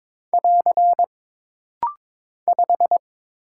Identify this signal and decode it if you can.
Morse code, heard as a keyed tone, about 22 wpm, 700 Hz. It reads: <AR> 5
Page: This practice set is composed of Letters, Numbers, common punctuations, and common prosigns.